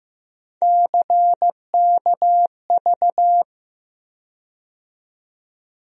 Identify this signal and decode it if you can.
Morse code, heard as a keyed tone, about 15 wpm, 700 Hz. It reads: CKV
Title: USA Call Sign Postfixes - 3 Letters 15wpm